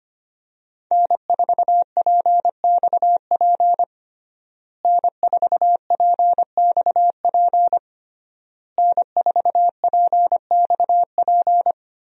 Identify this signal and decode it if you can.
Morse code, heard as a keyed tone, about 25 wpm, 700 Hz. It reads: N4PXP N4PXP N4PXP
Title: USA Call Signs - 5 Characters - Repeated 3x - 25wpm